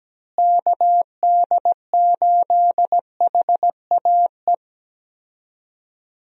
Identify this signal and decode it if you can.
Morse code, heard as a keyed tone, about 17 wpm, 700 Hz. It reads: KD8HAE
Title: USA Call Signs - 6 Characters - Repeated 3x - 17wpm